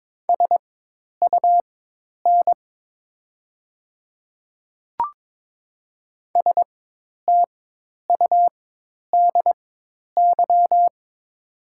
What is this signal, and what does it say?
Morse code, heard as a keyed tone, about 22 wpm, spaced as for 10 wpm, 700 Hz. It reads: SUN STUDY
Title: 4x Character Spacing - Top 300 Words - 22wpm